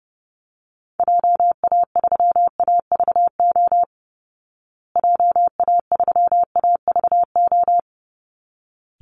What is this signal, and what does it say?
Morse code, heard as a keyed tone, about 30 wpm, 700 Hz. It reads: JA3AVO JA3AVO
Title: Call Signs Repeated 3x - 30wpm